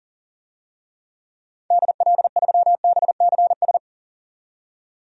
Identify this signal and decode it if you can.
Morse code, heard as a keyed tone, about 40 wpm, 700 Hz. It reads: DL3BCS
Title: DXCC-100 Call Signs - 6 Characters - 40wpm